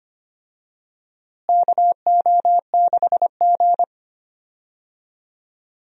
Morse code, heard as a keyed tone, about 25 wpm, 700 Hz. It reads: KO6G
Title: USA Call Signs - 4 Characters - Repeated 3x - 25wpm